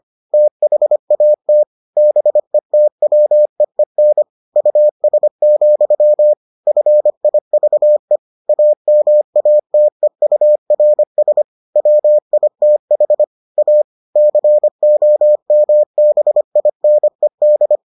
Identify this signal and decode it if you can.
Morse code, heard as a keyed tone, about 25 wpm, 600 Hz. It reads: THAT BETWEEN US, FIVE AMATEURS WITH A COMBINED